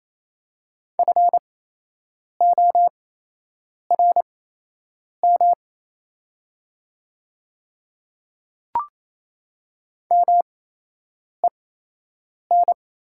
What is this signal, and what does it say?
Morse code, heard as a keyed tone, about 28 wpm, spaced as for 8 wpm, 700 Hz. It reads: FORM MEN
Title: 8x Character Spacing - Top 200 Words - 28wpm